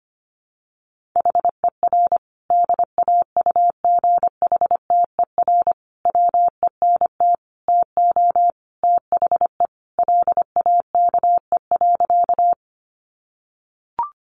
Morse code, heard as a keyed tone, about 25 wpm, 700 Hz. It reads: HER DAUGHTER WENT TO THE LAKE.
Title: Sentences from Top 1500 Words 25wpm